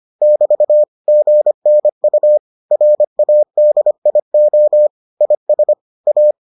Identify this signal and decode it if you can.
Morse code, heard as a keyed tone, about 25 wpm, 600 Hz. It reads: <BT> GNU RADIO IS A